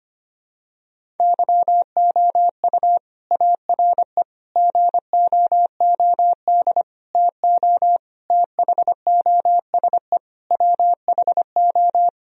Morse code, heard as a keyed tone, about 25 wpm, 700 Hz. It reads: YOU ARE GOOD TO THOSE WHO
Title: Sentences from Top 1000 Words - Review Part 1 - 25wpm